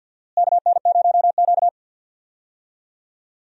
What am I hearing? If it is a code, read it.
Morse code, heard as a keyed tone, about 47 wpm, 700 Hz. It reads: KN0Q
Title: USA Call Signs - 4 Characters - Repeated 3x - 50wpm